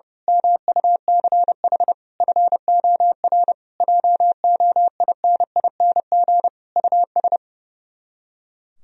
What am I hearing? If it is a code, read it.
Morse code, heard as a keyed tone, about 28 wpm, 700 Hz. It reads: MUCH FOR JOINING US
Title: Speed-Racing - Sets of 5 Words 30wpm